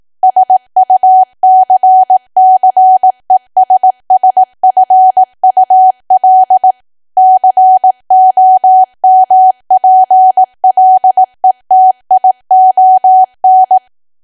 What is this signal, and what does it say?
Morse code, heard as a keyed tone, about 18 wpm, 750 Hz. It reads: SUCCESSFUL COMPLETION
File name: cwa / W1AW-Level2 / 18WPM